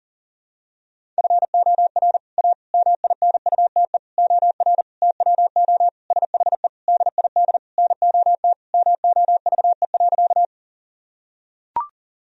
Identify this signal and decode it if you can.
Morse code, heard as a keyed tone, about 40 wpm, 700 Hz. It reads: FOR A MINUTE OR TWO SHE DID NOT MOVE.
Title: Sentences from Top 1000 Words - Review Part 1 - 40wpm